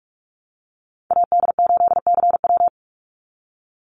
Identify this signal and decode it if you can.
Morse code, heard as a keyed tone, about 44 wpm, 700 Hz. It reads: AD8CW
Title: Call Signs Repeated 3x - 45wpm